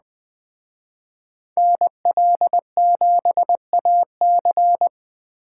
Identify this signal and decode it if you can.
Morse code, heard as a keyed tone, about 20 wpm, 700 Hz. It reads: NL7AC
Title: Speed Racing - USA 2x2 Call Signs 17wpm